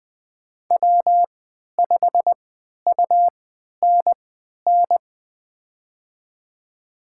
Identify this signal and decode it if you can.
Morse code, heard as a keyed tone, about 20 wpm, spaced as for 11 wpm, 700 Hz. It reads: W5UNN
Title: USA Call Signs - 5 Characters - 3x Character Spacing - 20wpm